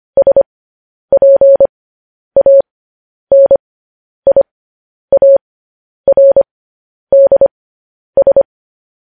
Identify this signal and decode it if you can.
Morse code, heard as a keyed tone, about 25 wpm, spaced as for 10 wpm, 550 Hz. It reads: SPANIARDS
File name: podcast / News headlines-1x-25x10